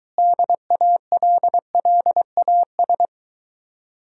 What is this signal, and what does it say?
Morse code, heard as a keyed tone, about 23 wpm, 700 Hz. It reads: DALLAS